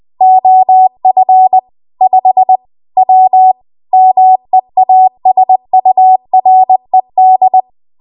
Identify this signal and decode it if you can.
Morse code, heard as a keyed tone, about 20 wpm, 750 Hz. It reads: OF 5 W MEASURED
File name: Archive / 20 WPM / 20WPM